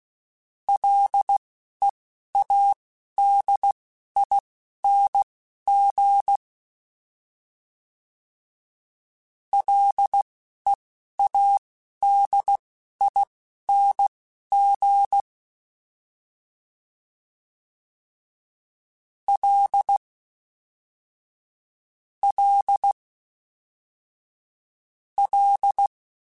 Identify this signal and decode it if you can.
Morse code, heard as a keyed tone, about 16 wpm, spaced as for 11 wpm, 800 Hz. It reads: LEADING LEADING L L L